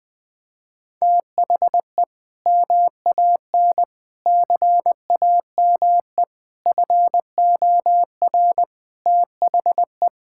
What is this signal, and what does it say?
Morse code, heard as a keyed tone, about 20 wpm, 700 Hz. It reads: THE MAN CAME FOR THE
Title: Sentences from Top 1000 Words - Review Part 5 - 20wpm